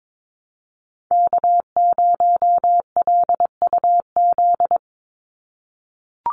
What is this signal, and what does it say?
Morse code, heard as a keyed tone, about 22 wpm, 700 Hz. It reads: K0LUZ
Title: Call Signs - 22wpm